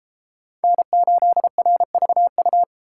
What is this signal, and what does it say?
Morse code, heard as a keyed tone, about 31 wpm, 700 Hz. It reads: N8RVU